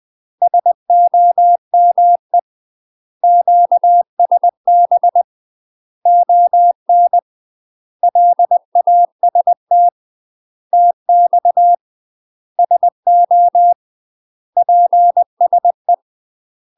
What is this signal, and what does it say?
Morse code, heard as a keyed tone, about 20 wpm, 700 Hz. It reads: SOME QSB ON LAST TX SO PSE
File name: UK MASTER QSOs TRAINING/20WPM